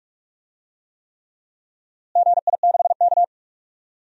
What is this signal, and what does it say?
Morse code, heard as a keyed tone, about 42 wpm, 700 Hz. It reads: GI6K
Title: DXCC-100 Call Signs - 4 Characters - Repeated 3x - 45wpm